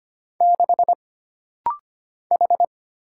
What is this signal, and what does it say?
Morse code, heard as a keyed tone, about 25 wpm, 700 Hz. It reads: 6 H
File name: Single Character (Letter:Number:Punctuation:Prosign) - Rapid-Fire - 25wpm